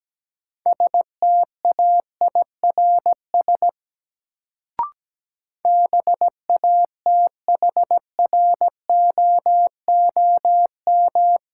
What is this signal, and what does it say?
Morse code, heard as a keyed tone, about 17 wpm, 700 Hz. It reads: STAIRS BATHROOM
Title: Sets of 5 Words Encore - New Word Drill - 17wpm